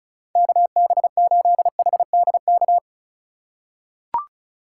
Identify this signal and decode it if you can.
Morse code, heard as a keyed tone, about 34 wpm, 700 Hz. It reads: KB8HDK
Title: USA Call Signs - 6 Characters - 2x Character Spacing - 35wpm